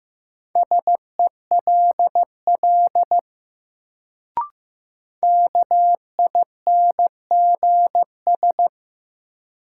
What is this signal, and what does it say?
Morse code, heard as a keyed tone, about 15 wpm, 700 Hz. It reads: SELL KINGS
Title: Sentences from Top 1000 Words - Review Part 6 - New Word Drill - 15wpm